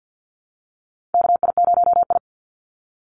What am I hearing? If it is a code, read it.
Morse code, heard as a keyed tone, about 50 wpm, 700 Hz. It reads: KI0I
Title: Call Signs Repeated 2x - 50wpm